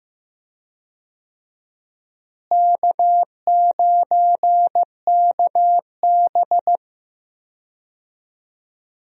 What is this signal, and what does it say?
Morse code, heard as a keyed tone, about 15 wpm, 700 Hz. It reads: K9KB
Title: USA Call Signs - 4 Characters - Repeated 3x - 15wpm